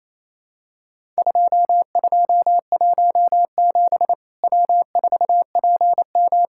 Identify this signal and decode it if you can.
Morse code, heard as a keyed tone, about 28 wpm, 700 Hz. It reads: 2217 W4PM
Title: Worked All Europe DX Contest - QTC Exchange - 28wpm